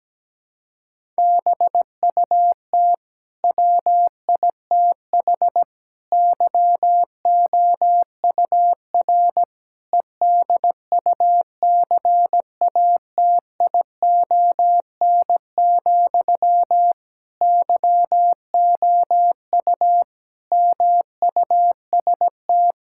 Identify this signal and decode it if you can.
Morse code, heard as a keyed tone, about 17 wpm, 700 Hz. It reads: BUT WITH YOUR EDUCATION, YOU MUST
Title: Sentences from Top 1000 Words - Review Part 6 - 17wpm